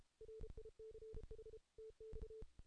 Morse code, heard as a keyed tone, about 32 wpm, 440 Hz. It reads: RICH TX
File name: 2016 JAN CW NAQP / 2016 Jan naqp k5ot - 0-8